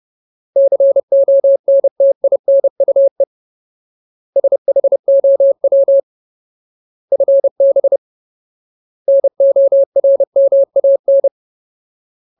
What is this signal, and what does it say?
Morse code, heard as a keyed tone, about 30 wpm, 550 Hz. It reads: CONTINUE SHOW FB NORMAN
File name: Ultimate Word List / 30WPM Ultimate